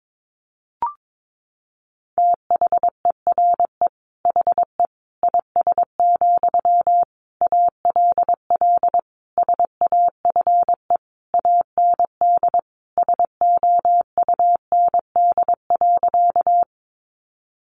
Morse code, heard as a keyed tone, about 22 wpm, 700 Hz. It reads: THERE HE IS, ALL SAFE AND SOUND.
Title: Sentences from Top 900 Words 22wpm